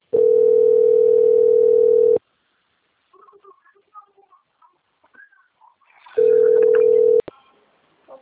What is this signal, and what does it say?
Ringback tone (North America), 440 + 480 Hz.
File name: out- - - - - .68992